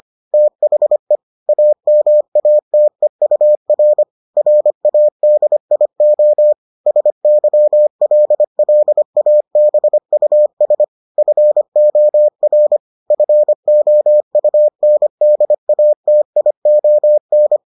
Morse code, heard as a keyed tone, about 25 wpm, 600 Hz. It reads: THE AMATEUR RADIO SYLLABUS FOR FOUNDATION